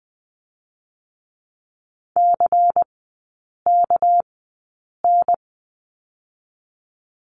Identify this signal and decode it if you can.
Morse code, heard as a keyed tone, about 20 wpm, 700 Hz. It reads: C K N
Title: 198 Lesson 32e - Letter k - Three Characters 20wpm